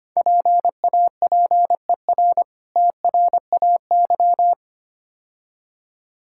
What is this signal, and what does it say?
Morse code, heard as a keyed tone, about 27 wpm, 690 Hz. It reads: PAPER TRAY
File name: practice / 25wpm / Difficult Words - Speed-Racing with Hints - 25wpm